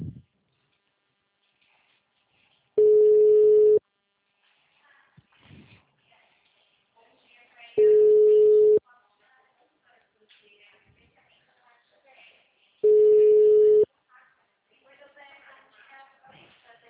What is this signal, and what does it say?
Ringback tone (Europe, 425 Hz), with speech.